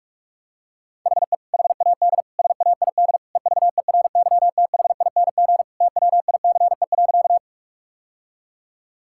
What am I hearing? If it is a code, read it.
Morse code, heard as a keyed tone, about 48 wpm, 700 Hz. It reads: HE HAD SAID EVERYTHING TWICE.
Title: Sentences from Top 1600 Words 45wpm